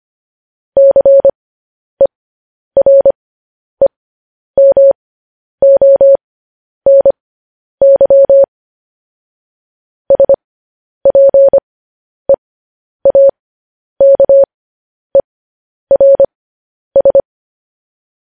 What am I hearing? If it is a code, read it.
Morse code, heard as a keyed tone, about 25 wpm, spaced as for 10 wpm, 550 Hz. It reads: CEREMONY SPEAKERS